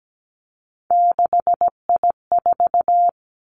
Morse code, heard as a keyed tone, about 17 wpm, 700 Hz. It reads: 6I4
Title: Three Challenging Letter-Number Combinations 17wpm